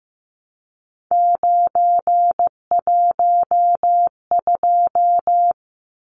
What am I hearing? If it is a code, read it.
Morse code, heard as a keyed tone, about 15 wpm, 700 Hz. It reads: 912
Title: Numbers - 3-Digit 15wpm